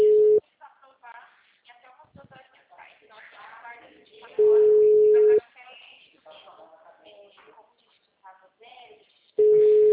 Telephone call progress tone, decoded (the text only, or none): ringback tone